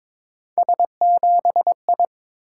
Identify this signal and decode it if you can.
Morse code, heard as a keyed tone, about 22 wpm, 700 Hz. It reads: S7I
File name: practice / 22wpm / Three Letter-Number Combinations - 22wpm